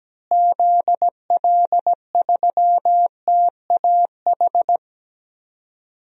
Morse code, heard as a keyed tone, about 17 wpm, 700 Hz. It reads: ZL3TAH
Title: DXCC-100 Call Signs - 6 Characters - Repeated 3x - 17wpm